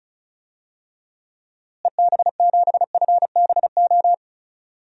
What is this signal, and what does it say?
Morse code, heard as a keyed tone, about 36 wpm, 700 Hz. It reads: EB7FBO
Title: DXCC-100 Call Signs - 6 Characters - Repeated 2x - 35wpm